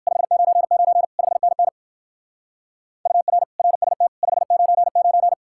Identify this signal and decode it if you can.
Morse code, heard as a keyed tone, about 60 wpm, 700 Hz. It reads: 599 5NN UR RST 599